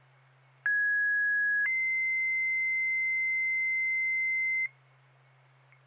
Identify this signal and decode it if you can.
Two-tone page, heard as a one-second tone, then a three-second tone, about 1650 then 2050 Hz, with noise. It is heard from 0.7 to 4.7 s.